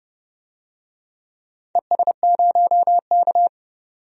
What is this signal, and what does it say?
Morse code, heard as a keyed tone, about 30 wpm, 700 Hz. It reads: ES0K
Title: DXCC-100 Call Signs - 4 Characters - 30wpm